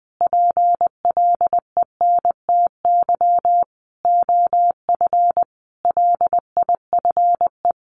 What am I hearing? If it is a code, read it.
Morse code, heard as a keyed tone, about 20 wpm, 700 Hz. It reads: PLENTY OF LIFE
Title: Sentences from Top 1300 Words 20wpm